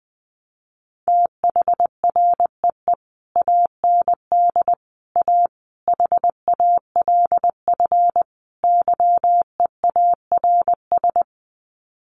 Morse code, heard as a keyed tone, about 20 wpm, 700 Hz. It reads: THREE AND A HALF YEARS
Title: Sets of 5 Words Encore2 20wpm